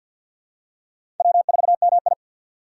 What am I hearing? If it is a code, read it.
Morse code, heard as a keyed tone, about 51 wpm, 700 Hz. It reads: W4MI
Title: USA Call Signs - 2x Character Spacing - 50wpm